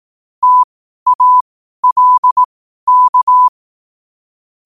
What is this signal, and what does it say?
Morse code, heard as a keyed tone, about 18 wpm, spaced as for 12 wpm, 1000 Hz. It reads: TALK
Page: Quotes for Wed, 13 Aug 2025 in Morse Code at 12 words per minute.